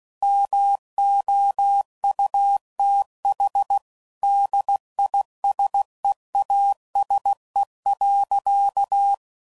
Morse code, heard as a keyed tone, about 16 wpm, 800 Hz. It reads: MOUTH DISEASE.
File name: Sayings / Quotes / HoofInMouth